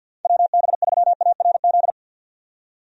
Morse code, heard as a keyed tone, about 52 wpm, 700 Hz. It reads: WB3ARZ
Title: USA Call Signs - 6 Characters - 50wpm